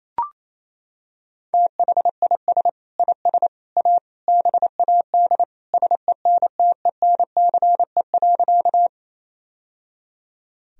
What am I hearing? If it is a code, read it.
Morse code, heard as a keyed tone, about 28 wpm, 700 Hz. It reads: THIS IS A BAD SENTENCE.
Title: Sentences from Top 1000 Words - Review Part 1 - 28wpm